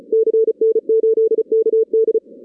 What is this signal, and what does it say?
Morse code, heard as a keyed tone, about 36 wpm, 440 Hz. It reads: CN8KD
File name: 2016 arrl dx cw / 2016-ARRL-DX-CW-K5TR- - 0-17